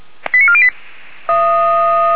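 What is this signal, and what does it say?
Five-tone selective call (ZVEI), heard as quick runs of five tones, tones of about 70 ms. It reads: 89408
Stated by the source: ZVEI five-tone selective call 89406 received by SDR.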